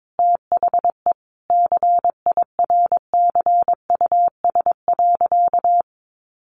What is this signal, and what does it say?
Morse code, heard as a keyed tone, about 22 wpm, 700 Hz. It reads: THE CIRCUS.
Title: Easy Phrases 22wpm